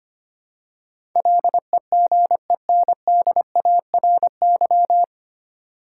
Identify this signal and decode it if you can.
Morse code, heard as a keyed tone, about 25 wpm, 700 Hz. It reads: LEGENDARY
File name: practice / 25wpm / 9 Letter Words - First 3 Letters Given - 25wpm